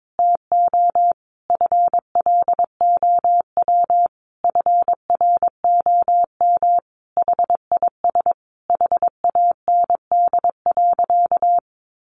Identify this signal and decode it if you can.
Morse code, heard as a keyed tone, about 22 wpm, 700 Hz. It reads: TO FLOW FROM HIS HAND.
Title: Sentences from Top 1200 Words 22wpm